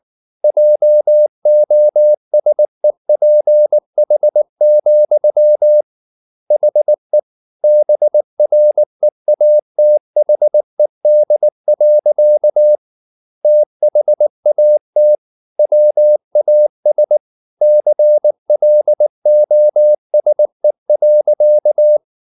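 Morse code, heard as a keyed tone, about 19 wpm, 600 Hz. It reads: JOSEPH, HE BREATHED. THAT WAS CLOSE.
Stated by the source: Morse code at 19 wpm from CW med Gnister, reading Ken Follett's novel Never in English.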